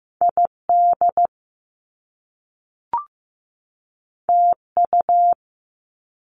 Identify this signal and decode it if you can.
Morse code, heard as a keyed tone, about 15 wpm, 700 Hz. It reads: ID TU
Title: Top Two Letter Combinations in Words 15wpm